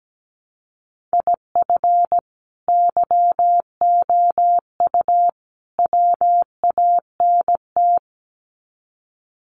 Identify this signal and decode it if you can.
Morse code, heard as a keyed tone, about 17 wpm, 700 Hz. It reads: IF YOU WANT
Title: Sets of 3 Words 17wpm